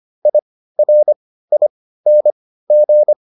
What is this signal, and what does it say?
Morse code, heard as a keyed tone, about 25 wpm, spaced as for 15 wpm, 600 Hz. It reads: IRING